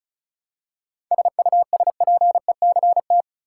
Morse code, heard as a keyed tone, about 36 wpm, 700 Hz. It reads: SUSPECT
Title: Seven Letter Words - 35wpm